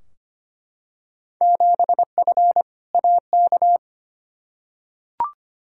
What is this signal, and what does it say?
Morse code, heard as a keyed tone, about 25 wpm, 700 Hz. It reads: 7F AK
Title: Field Day Exchanges Encore - 25wpm